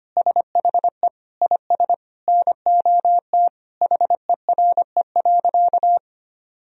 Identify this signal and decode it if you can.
Morse code, heard as a keyed tone, about 25 wpm, 700 Hz. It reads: SHE IS NOT HERE.
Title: Sentences from Top 1000 Words - Review Part 3 - 25wpm